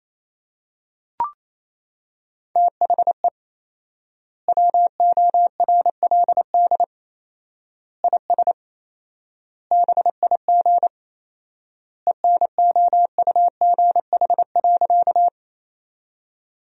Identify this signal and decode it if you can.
Morse code, heard as a keyed tone, about 27 wpm, 700 Hz. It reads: THE WORLD IS BIG ENOUGH.
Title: 4x Word-Spacing - Sentences from Top 400 Words - 28wpm